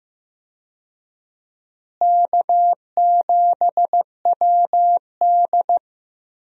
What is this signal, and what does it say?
Morse code, heard as a keyed tone, about 15 wpm, 700 Hz. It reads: K7WD
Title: USA Call Signs - 4 Characters - Repeated 3x - 15wpm